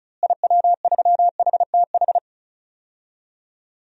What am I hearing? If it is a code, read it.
Morse code, heard as a keyed tone, about 34 wpm, 700 Hz. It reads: IW3HTH
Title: DXCC-100 Call Signs - 6 Characters - Repeated 3x - 35wpm